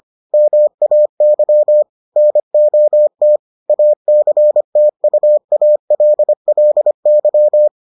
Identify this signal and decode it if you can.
Morse code, heard as a keyed tone, about 25 wpm, 600 Hz. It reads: MAY NOT ACTUALLY